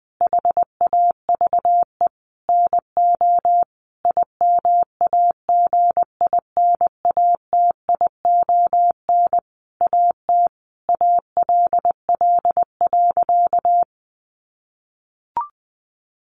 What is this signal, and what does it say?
Morse code, heard as a keyed tone, about 20 wpm, 700 Hz. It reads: HAVE NO IMAGINATION AT ALL.
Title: Sentences from Top 1300 Words 20wpm